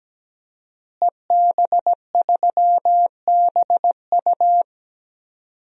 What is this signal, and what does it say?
Morse code, heard as a keyed tone, about 17 wpm, 700 Hz. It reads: EB3BU
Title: DXCC-100 Call Signs - 5 Characters - Repeated 2x - 17wpm